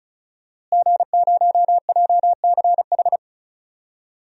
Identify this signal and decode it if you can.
Morse code, heard as a keyed tone, about 34 wpm, 700 Hz. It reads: G0JCH
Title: DXCC-100 Call Signs - 5 Characters - Repeated 2x - 35wpm